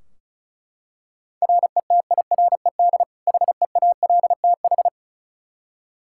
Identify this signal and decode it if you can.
Morse code, heard as a keyed tone, about 36 wpm, 700 Hz. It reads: RETIRED HEALTH
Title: Practice QSO - Exchange 3 - More Info - 35wpm